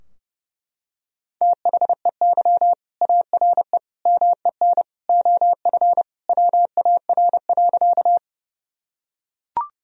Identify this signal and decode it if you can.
Morse code, heard as a keyed tone, about 30 wpm, 700 Hz. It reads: THEY ARE MEN OF WAR.
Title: Sentences from Top 1000 Words - Review Part 2 - 30wpm